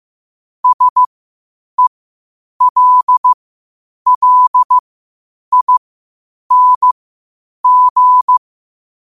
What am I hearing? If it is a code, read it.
Morse code, heard as a keyed tone, about 15 wpm, spaced as for 8 wpm, 1000 Hz. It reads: SELLING
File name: qotdfiles / 8wpm